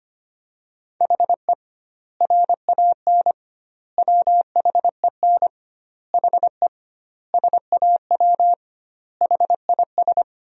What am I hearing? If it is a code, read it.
Morse code, heard as a keyed tone, about 25 wpm, 700 Hz. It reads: HE RAN WHEN HE SAW HIS
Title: 2x Word-Spacing - Sentences from Top 500 Words - 25wpm